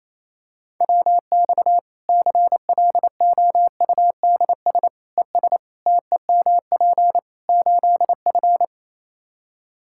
Morse code, heard as a keyed tone, about 28 wpm, 700 Hz. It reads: WX CLOUDS ES TEMP 8F
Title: Practice QSO - Exchange 2 - Info - 28wpm